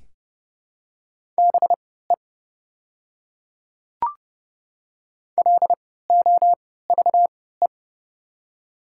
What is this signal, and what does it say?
Morse code, heard as a keyed tone, about 30 wpm, spaced as for 17 wpm, 700 Hz. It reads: BE LOVE
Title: 3x Character Spacing - Top 500 Words - 30wpm